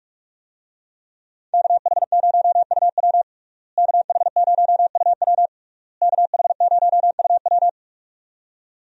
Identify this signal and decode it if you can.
Morse code, heard as a keyed tone, about 41 wpm, 700 Hz. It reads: KH0UW KH0UW KH0UW
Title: USA Call Signs - 5 Characters - Repeated 3x - 45wpm